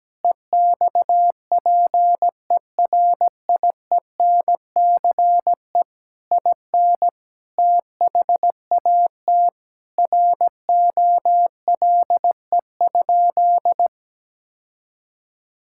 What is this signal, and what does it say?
Morse code, heard as a keyed tone, about 17 wpm, 700 Hz. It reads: EXPERIENCE IN THAT ROLE?
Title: Sentences from Top 1000 Words - Review Part 6 - 17wpm